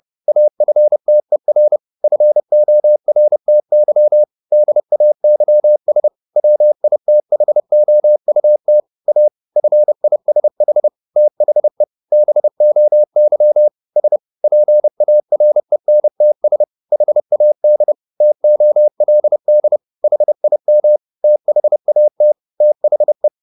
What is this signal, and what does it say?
Morse code, heard as a keyed tone, about 30 wpm, 600 Hz. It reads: AFTER FORTY DAYS WITHOUT A FISH THE BOY S PARENTS HAD TOLD HIM THAT THE
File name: the old man and the sea / the old man and the sea 1 30wpm